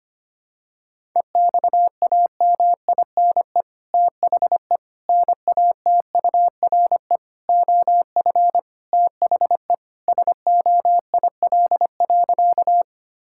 Morse code, heard as a keyed tone, about 25 wpm, 700 Hz. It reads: EXAMINE THE NATURE OF THE SOIL.
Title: Sentences from Top 1600 Words 25wpm